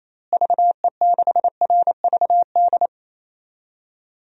Morse code, heard as a keyed tone, about 28 wpm, 700 Hz. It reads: VE6RVD
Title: DXCC-100 Call Signs - 6 Characters - Repeated 3x - 28wpm